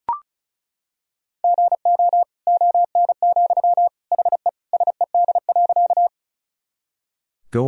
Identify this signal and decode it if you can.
Morse code, heard as a keyed tone, about 36 wpm, 700 Hz. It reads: GO ON, HE SED.
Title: Sentences from Top 100 Words and Abbreviations 35wpm